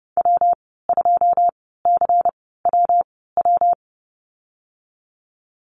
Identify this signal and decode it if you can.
Morse code, heard as a keyed tone, about 30 wpm, spaced as for 17 wpm, 700 Hz. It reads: W2CWW
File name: Lesson 19g - Letter w - Call Signs 30wpm